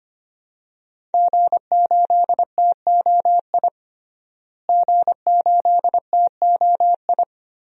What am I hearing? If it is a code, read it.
Morse code, heard as a keyed tone, about 25 wpm, 700 Hz. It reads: G8TOI G8TOI
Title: DXCC-100 Call Signs - 5 Characters - Repeated 3x - 25wpm